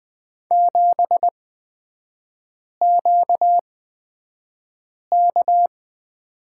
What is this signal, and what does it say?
Morse code, heard as a keyed tone, about 20 wpm, spaced as for 5 wpm, 690 Hz. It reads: 7QK
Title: 236 Lesson 38e - Letter q - Three Characters 20wpm